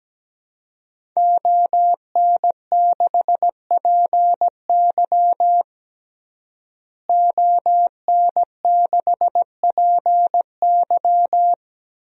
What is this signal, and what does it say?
Morse code, heard as a keyed tone, about 17 wpm, 700 Hz. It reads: ON6PY ON6PY
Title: DXCC-100 Call Signs - 5 Characters - Repeated 2x - 17wpm